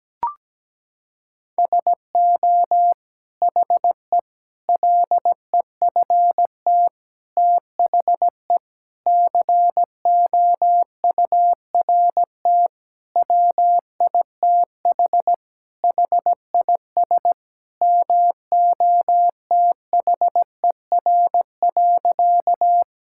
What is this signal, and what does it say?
Morse code, heard as a keyed tone, about 17 wpm, 700 Hz. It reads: SO HE LEFT THE COURT WITH HIS MOTHER.
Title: Sentences from Top 1600 Words 17wpm